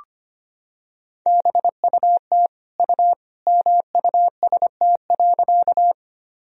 Morse code, heard as a keyed tone, about 25 wpm, 700 Hz. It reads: BUT U MUST.
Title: Sentences from Top 200 Words and Abbreviations 25wpm